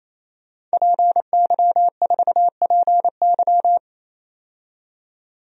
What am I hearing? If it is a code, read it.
Morse code, heard as a keyed tone, about 28 wpm, 700 Hz. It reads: PY4PY
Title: DXCC-100 Call Signs - 5 Characters - 28wpm